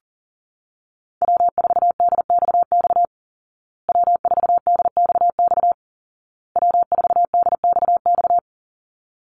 Morse code, heard as a keyed tone, about 40 wpm, 700 Hz. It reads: W4DXX W4DXX W4DXX
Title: Call Signs Repeated 2x - 40wpm